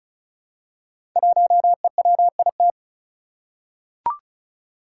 Morse code, heard as a keyed tone, about 34 wpm, 700 Hz. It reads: 1EWIT
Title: Five Challenging Letter-Number Combinations 35wpm